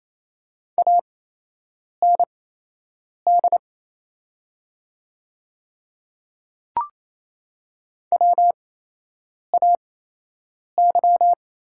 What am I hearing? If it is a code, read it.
Morse code, heard as a keyed tone, about 28 wpm, spaced as for 8 wpm, 700 Hz. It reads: AND WAY
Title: 8x Character Spacing - Top 100 Words - 28wpm